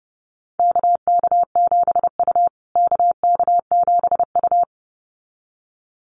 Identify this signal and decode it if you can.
Morse code, heard as a keyed tone, about 30 wpm, 700 Hz. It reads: KK7U KK7U
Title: USA Call Signs Repeated 2x 30wpm